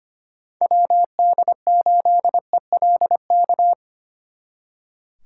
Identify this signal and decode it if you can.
Morse code, heard as a keyed tone, about 29 wpm, 700 Hz. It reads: WD8ELK
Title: Speed Racing - USA 2x3 Call Signs 25wpm